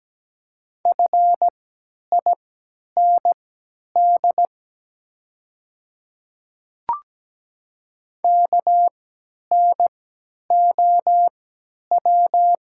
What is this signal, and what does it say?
Morse code, heard as a keyed tone, about 17 wpm, spaced as for 10 wpm, 690 Hz. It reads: FIND KNOW
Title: 3x Character Spacing - Top 200 Words - 17wpm